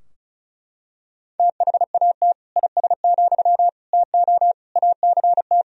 Morse code, heard as a keyed tone, about 36 wpm, 700 Hz. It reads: THAT IS, TO ACT
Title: Sentences from Top 1000 Words - Review Part 4 - 35wpm